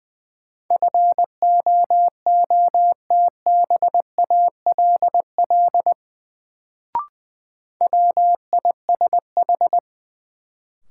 Morse code, heard as a keyed tone, about 20 wpm, 700 Hz. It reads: FOOTBALL WISH
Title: Sentences from Top 1600 Words - New Word Drill - 20wpm